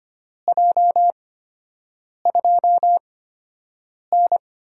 Morse code, heard as a keyed tone, about 25 wpm, spaced as for 7 wpm, 700 Hz. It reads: J2N
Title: USA Call Signs - 4 Characters - 8x Character Spacing - 25wpm